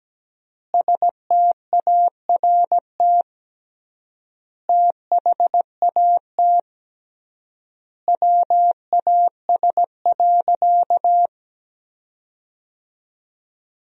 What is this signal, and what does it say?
Morse code, heard as a keyed tone, about 17 wpm, 700 Hz. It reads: START THAT WAS.
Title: 3x Word-Spacing - Sentences from Top 300 Words - 17wpm